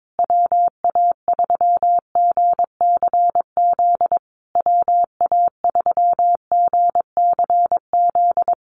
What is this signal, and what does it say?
Morse code, heard as a keyed tone, about 22 wpm, 690 Hz. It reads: WA3GCZ WA3GCZ
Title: USA Call Signs Repeated 3x 22wpm